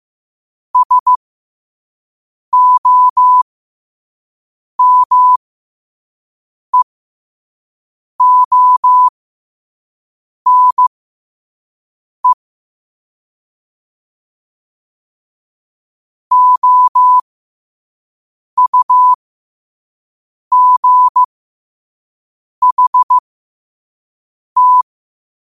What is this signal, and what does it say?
Morse code, heard as a keyed tone, about 15 wpm, spaced as for 5 wpm, 1000 Hz. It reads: SOMEONE OUGHT
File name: qotdfiles / 5wpm